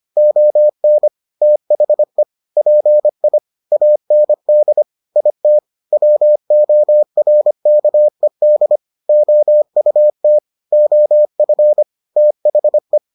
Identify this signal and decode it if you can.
Morse code, heard as a keyed tone, about 25 wpm, 600 Hz. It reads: ON THE PI AND IT WORKED OUT OF THE